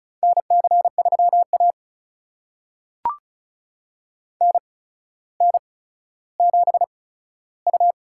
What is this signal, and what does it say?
Morse code, heard as a keyed tone, about 36 wpm, spaced as for 15 wpm, 700 Hz. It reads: NC3A NN7U
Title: USA Call Signs - 4 Characters - 8x Character Spacing - 35wpm